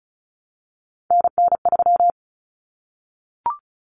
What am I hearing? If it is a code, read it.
Morse code, heard as a keyed tone, about 34 wpm, 700 Hz. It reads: NN3
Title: Common Cut Numbers - 3-Digit 35wpm